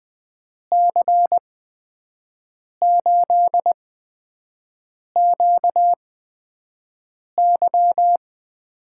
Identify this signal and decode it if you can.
Morse code, heard as a keyed tone, about 20 wpm, spaced as for 5 wpm, 700 Hz. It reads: C8QY
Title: USA Call Signs - 5 Characters - 8x Character Spacing - 20wpm